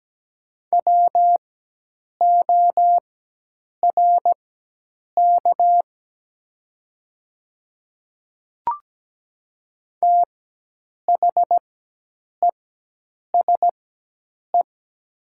Morse code, heard as a keyed tone, about 17 wpm, spaced as for 8 wpm, 700 Hz. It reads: WORK THESE
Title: 4x Character Spacing - Top 200 Words - 17wpm